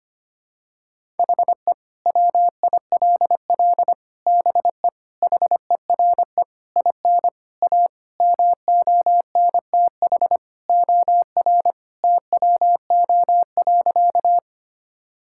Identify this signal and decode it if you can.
Morse code, heard as a keyed tone, about 25 wpm, 700 Hz. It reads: HE WILL BE HERE IN A MONTH OR TWO.
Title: Sentences from Top 1000 Words - Review Part 5 - 25wpm